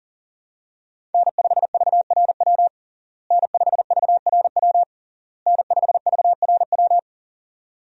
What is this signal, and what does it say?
Morse code, heard as a keyed tone, about 37 wpm, 700 Hz. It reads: N5VRW N5VRW N5VRW
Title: USA Call Signs - 5 Characters - Repeated 3x - 40wpm